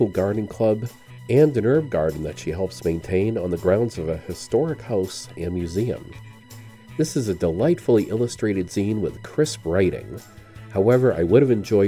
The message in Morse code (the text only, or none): none